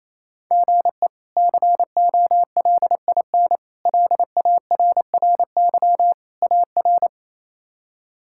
Morse code, heard as a keyed tone, about 28 wpm, 700 Hz. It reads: GE COLIN LARRY AR
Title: K1USN SST QSO Practice - 28wpm